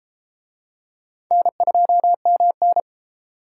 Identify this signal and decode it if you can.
Morse code, heard as a keyed tone, about 32 wpm, 700 Hz. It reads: N2MN